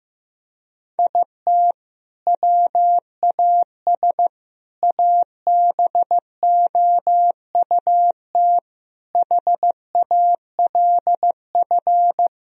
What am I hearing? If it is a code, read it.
Morse code, heard as a keyed tone, about 15 wpm, 700 Hz. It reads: IT WAS ABOUT HALF
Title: Sentences from Top 1000 Words - Review Part 3 - 15wpm